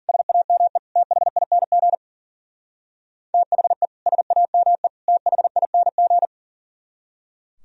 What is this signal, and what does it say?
Morse code, heard as a keyed tone, about 48 wpm, 700 Hz. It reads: SAME THING THE SAME THING
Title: Speed-Racing - Sets of 3 Words 40wpm